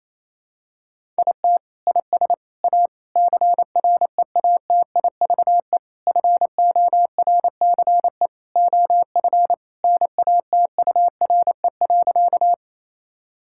Morse code, heard as a keyed tone, about 28 wpm, 700 Hz. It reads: IT IS A CREATIVE FORCE OF NATURE.
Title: Sentences from Top 1000 Words - Review Part 5 - 28wpm